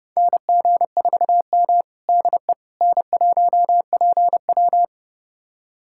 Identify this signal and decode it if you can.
Morse code, heard as a keyed tone, about 30 wpm, 700 Hz. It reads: NG4M DE N1PW
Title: USA Callsign DE Callsign - 4 Characters - 30wpm